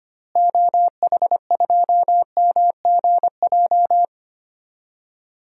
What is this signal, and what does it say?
Morse code, heard as a keyed tone, about 25 wpm, 700 Hz. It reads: OH2MGJ
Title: DXCC-100 Call Signs - 6 Characters - Repeated 3x - 25wpm